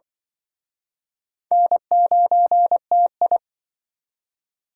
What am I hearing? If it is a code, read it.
Morse code, heard as a keyed tone, about 25 wpm, 700 Hz. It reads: N9TI